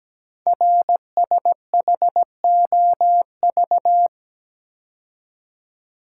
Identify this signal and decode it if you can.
Morse code, heard as a keyed tone, about 17 wpm, 700 Hz. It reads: RSHOV
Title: Code Groups - 5 Letter:Numbers - 17wpm